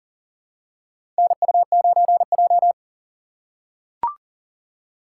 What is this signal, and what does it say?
Morse code, heard as a keyed tone, about 40 wpm, 700 Hz. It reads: NU9J
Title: USA Call Signs - 4 Characters - 6x Character Spacing - 40wpm